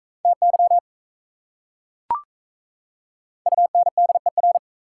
Morse code, heard as a keyed tone, about 35 wpm, 690 Hz. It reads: TY UNDER
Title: Speed-Racing Prefixes and Suffixes 28wpm